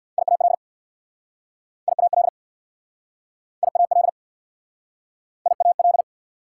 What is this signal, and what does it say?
Morse code, heard as a keyed tone, about 69 wpm, 690 Hz. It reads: IAL IAL IAL IAL